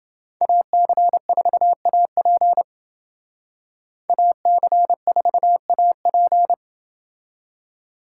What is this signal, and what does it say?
Morse code, heard as a keyed tone, about 28 wpm, 700 Hz. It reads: AC4AP AC4AP